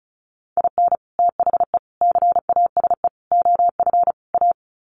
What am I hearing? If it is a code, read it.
Morse code, heard as a keyed tone, about 34 wpm, 700 Hz. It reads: IN THE CASE OF A
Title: Sets of 5 Words Encore 35wpm